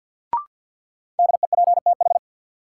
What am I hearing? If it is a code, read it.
Morse code, heard as a keyed tone, about 46 wpm, 700 Hz. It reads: DEPTH
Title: Sentences from Top 1200 Words - New Word Drill - 50wpm